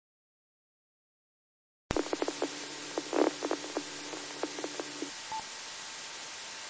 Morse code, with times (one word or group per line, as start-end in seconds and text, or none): none